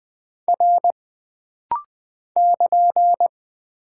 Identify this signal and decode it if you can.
Morse code, heard as a keyed tone, about 20 wpm, 700 Hz. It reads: R <KN>
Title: Single Character (Letter:Number:Punctuation:Prosign) - Rapid-Fire - 20wpm